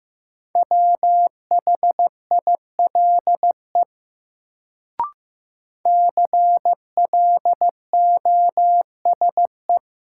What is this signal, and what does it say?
Morse code, heard as a keyed tone, about 15 wpm, 700 Hz. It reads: WHILE CLOSE
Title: Sentences from Top 300 Words - New Word Drill - 15wpm